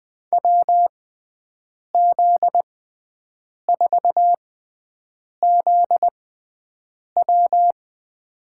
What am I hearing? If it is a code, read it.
Morse code, heard as a keyed tone, about 20 wpm, spaced as for 7 wpm, 700 Hz. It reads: WZ4ZW
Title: USA Call Signs - 5 Characters - 6x Character Spacing - 20wpm